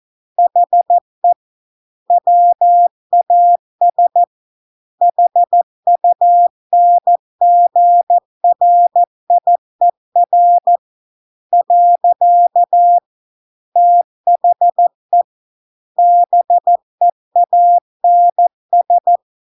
Morse code, heard as a keyed tone, about 14 wpm, 700 Hz. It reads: HE WAS HUNGRIER . THE BEANS